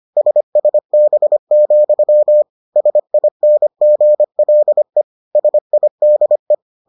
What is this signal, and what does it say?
Morse code, heard as a keyed tone, about 25 wpm, 600 Hz. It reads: SSB, SINGLE SIDE